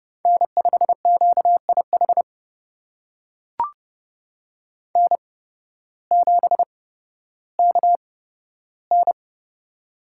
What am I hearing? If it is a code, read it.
Morse code, heard as a keyed tone, about 31 wpm, spaced as for 14 wpm, 700 Hz. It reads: N5QIH N7KN
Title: USA Call Signs - 8x Character Spacing - 30wpm